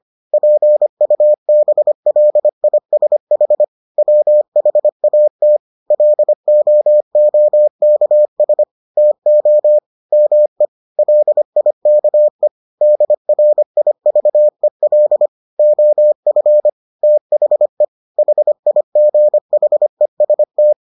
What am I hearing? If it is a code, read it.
Morse code, heard as a keyed tone, about 25 wpm, 600 Hz. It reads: PUBLISH WHAT LOOKS TO ME LIKE DRIVEL OF THE HIGHEST